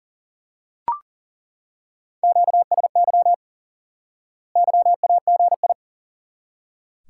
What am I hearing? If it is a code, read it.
Morse code, heard as a keyed tone, about 37 wpm, 700 Hz. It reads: QSY YAGI
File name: course / instant-qso-element-course-3 / IQE Course 3 - Lesson 89 - YAGI - Two Elements - 40wpm